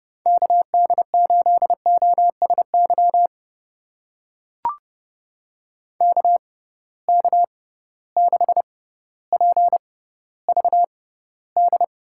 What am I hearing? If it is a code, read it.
Morse code, heard as a keyed tone, about 30 wpm, spaced as for 15 wpm, 700 Hz. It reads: KD8OSY KK6PVD
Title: USA Call Signs - 6x Character Spacing - 30wpm